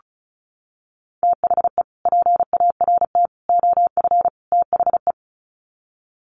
Morse code, heard as a keyed tone, about 34 wpm, 700 Hz. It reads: THE PART OF THE
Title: Sets of 4 Words 35wpm